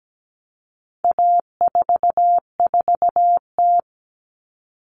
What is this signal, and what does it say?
Morse code, heard as a keyed tone, about 17 wpm, 700 Hz. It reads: A44T
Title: All Cut Numbers - 4-Digit 17wpm